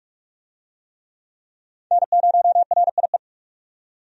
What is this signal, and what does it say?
Morse code, heard as a keyed tone, about 46 wpm, 700 Hz. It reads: N0RIE
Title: USA Call Signs - 5 Characters - 45wpm